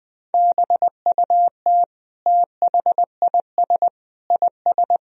Morse code, heard as a keyed tone, about 20 wpm, 700 Hz. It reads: BUT THIS IS
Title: Sentences from Top 1000 Words - Review Part 3 - 20wpm